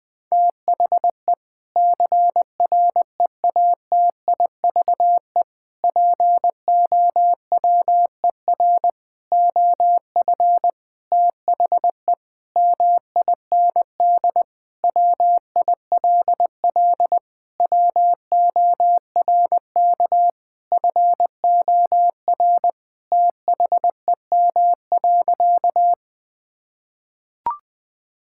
Morse code, heard as a keyed tone, about 20 wpm, 700 Hz. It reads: THE CREATIVE POWER OF THE MIND WILL WORK FOR THEM.
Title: Sentences from Top 1000 Words - Review Part 5 - 20wpm